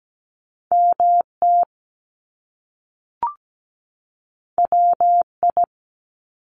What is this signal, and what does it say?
Morse code, heard as a keyed tone, about 17 wpm, 700 Hz. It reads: MT WI
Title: US States 17wpm